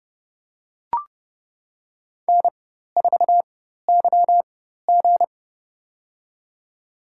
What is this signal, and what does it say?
Morse code, heard as a keyed tone, about 30 wpm, spaced as for 14 wpm, 700 Hz. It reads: N4YG
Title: USA Call Signs - 4x Character Spacing - 30wpm